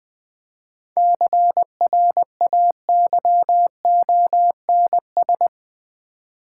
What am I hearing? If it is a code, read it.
Morse code, heard as a keyed tone, about 21 wpm, 700 Hz. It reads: CRAYONS
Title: Easy Words with Hints - 20wpm